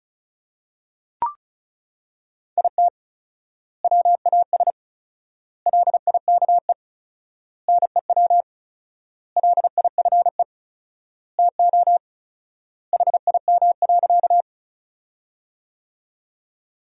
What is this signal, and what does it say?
Morse code, heard as a keyed tone, about 36 wpm, 700 Hz. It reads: IT WAS LIKE NEW LIFE TO HIM.
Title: 4x Word-Spacing - Sentences from Top 300 Words - 35wpm